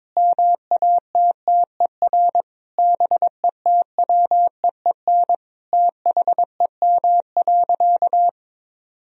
Morse code, heard as a keyed tone, about 22 wpm, 700 Hz. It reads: MATTER BETWEEN THEM.
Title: Sentences from Top 1000 Words - Review Part 4 - 22wpm